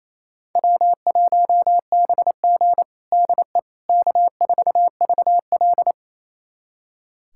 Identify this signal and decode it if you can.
Morse code, heard as a keyed tone, about 28 wpm, 700 Hz. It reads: W1BG DE K4VL
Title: USA Callsign DE Callsign - 4 Characters - 28wpm